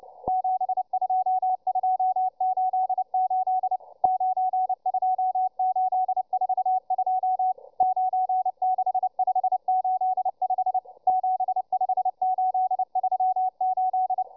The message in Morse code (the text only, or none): 72288 92842 96585 75838